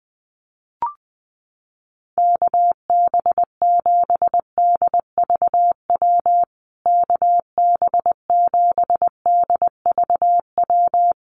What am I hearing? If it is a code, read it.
Morse code, heard as a keyed tone, about 20 wpm, 700 Hz. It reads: KB7DVW KB7DVW
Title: USA Call Signs Repeated 3x 20wpm